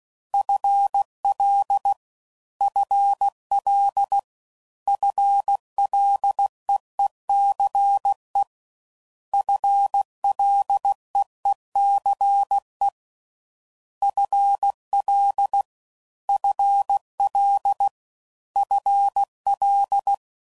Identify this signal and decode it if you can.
Morse code, heard as a keyed tone, about 16 wpm, 800 Hz. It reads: FL FL FLEECE FLEECE FL FL FL